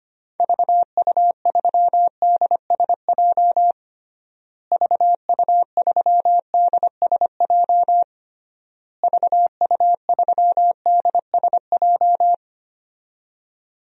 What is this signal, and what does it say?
Morse code, heard as a keyed tone, about 25 wpm, 700 Hz. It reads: VU3DSJ VU3DSJ VU3DSJ
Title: DXCC-100 Call Signs - 6 Characters - Repeated 3x - 25wpm